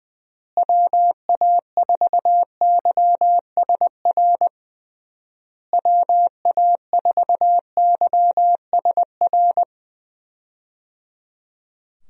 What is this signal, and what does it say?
Morse code, heard as a keyed tone, about 20 wpm, 700 Hz. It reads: WA4YSR WA4YSR
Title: USA Call Signs - 6 Characters - Repeated 3x - 20wpm